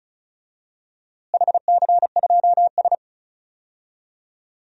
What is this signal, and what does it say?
Morse code, heard as a keyed tone, about 34 wpm, 700 Hz. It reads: HC2S
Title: DXCC-100 Call Signs - 4 Characters - Repeated 3x - 35wpm